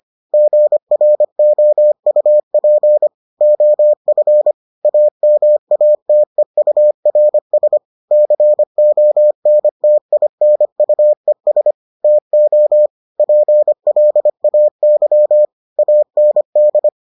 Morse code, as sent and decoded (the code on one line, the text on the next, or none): --. .-. --- ..- .--. / --- ..-. / .- -- .- - . ..- .-. ... / -.-. --- -. - .. -. ..- . ... / - --- / .--. .-.. .- -.-- / .- -. -..
GROUP OF AMATEURS CONTINUES TO PLAY AND